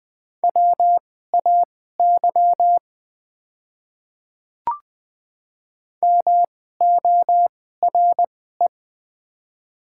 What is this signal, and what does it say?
Morse code, heard as a keyed tone, about 20 wpm, spaced as for 14 wpm, 690 Hz. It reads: WAY MORE
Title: 2x Character Spacing - Top 100 Words - 20wpm